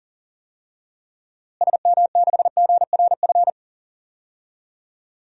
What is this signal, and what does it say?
Morse code, heard as a keyed tone, about 38 wpm, 700 Hz. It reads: SM6GRF
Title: DXCC-100 Call Signs - 6 Characters - Repeated 2x - 40wpm